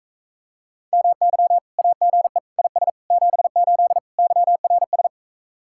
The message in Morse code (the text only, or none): MY AGE IS 78 YRS